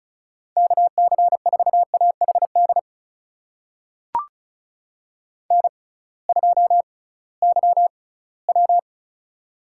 Morse code, heard as a keyed tone, about 36 wpm, spaced as for 20 wpm, 700 Hz. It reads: KC4AHD N2YW
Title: USA Call Signs - 6x Character Spacing - 35wpm